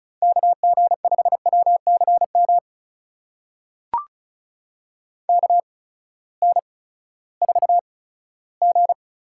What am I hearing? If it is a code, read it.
Morse code, heard as a keyed tone, about 34 wpm, spaced as for 17 wpm, 700 Hz. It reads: KG5WCM KN4G
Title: USA Call Signs - 8x Character Spacing - 35wpm